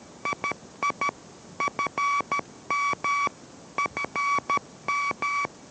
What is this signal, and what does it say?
Morse code, heard as a keyed tone, about 13 wpm, 1140 Hz, with noise. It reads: II FM ENM